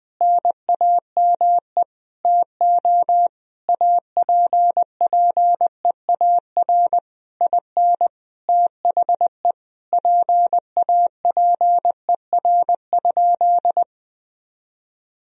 Morse code, heard as a keyed tone, about 20 wpm, 700 Hz. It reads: NAME TO APPEAR IN THE PAPER?
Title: Sentences from Top 1000 Words - Review Part 2 - 20wpm